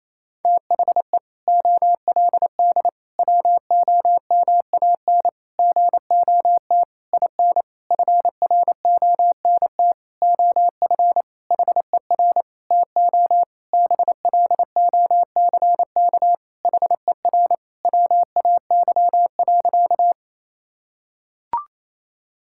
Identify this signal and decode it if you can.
Morse code, heard as a keyed tone, about 28 wpm, 700 Hz. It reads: THE OLD WOMAN GOT IN FRONT OF HER TO BLOCK HER WAY.
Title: Sentences from Top 1600 Words 28wpm